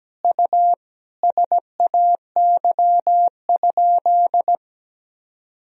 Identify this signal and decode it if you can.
Morse code, heard as a keyed tone, about 17 wpm, 700 Hz. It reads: U SAY?
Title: Sentences from Top 400 Words and Abbreviations 17wpm